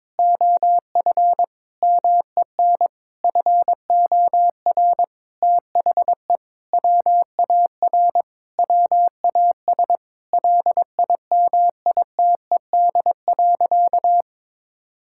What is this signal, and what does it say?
Morse code, heard as a keyed tone, about 22 wpm, 700 Hz. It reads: OF MEN FOR THE WAR WAS LIMITED.
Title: Sentences from Top 1000 Words - Review Part 6 - 22wpm